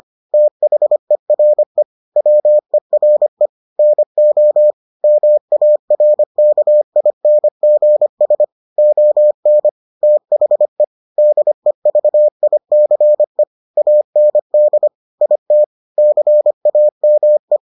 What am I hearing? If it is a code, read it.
Morse code, heard as a keyed tone, about 25 wpm, 600 Hz. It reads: THERE WERE NO MARKINGS ON THE DEVICE AND IT CAME